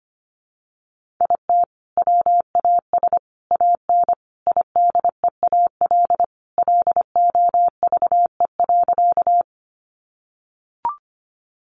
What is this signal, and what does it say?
Morse code, heard as a keyed tone, about 25 wpm, 700 Hz. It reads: IT WAS AN IDEAL LOVE.
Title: Sentences from Top 1400 Words 25wpm